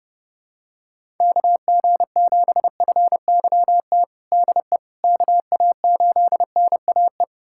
Morse code, heard as a keyed tone, about 31 wpm, 700 Hz. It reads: KG7FYT DE KA8NAE
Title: USA Callsign DE Callsign - 6 Characters - 30wpm